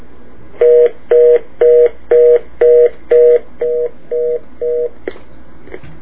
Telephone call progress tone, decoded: reorder tone